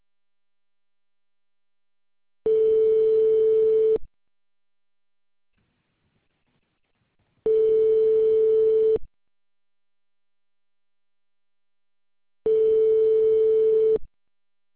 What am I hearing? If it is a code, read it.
Ringback tone (France), 440 Hz.